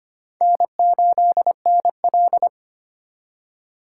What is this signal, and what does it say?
Morse code, heard as a keyed tone, about 32 wpm, 700 Hz. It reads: N8NL